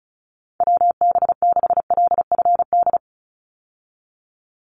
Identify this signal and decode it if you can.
Morse code, heard as a keyed tone, about 34 wpm, 700 Hz. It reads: WB6LFD
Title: USA Call Signs 35wpm